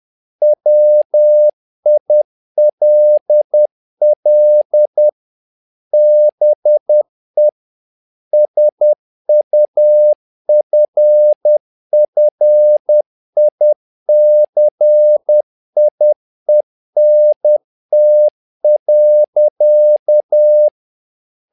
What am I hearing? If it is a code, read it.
Morse code, heard as a keyed tone, about 10 wpm, 600 Hz. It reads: WILL BE SUFFICIENT.
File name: learning center / tower trouble / tower trouble 10wpm